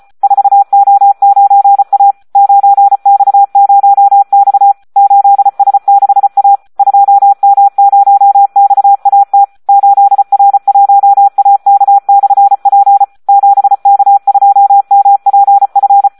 Morse code, heard as a keyed tone, about 34 wpm, 800 Hz. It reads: 4O9A 9X0X 8S6A 2M0XAT 8R1AK/P 7K2MPF
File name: Audio practica CW / SD 1 34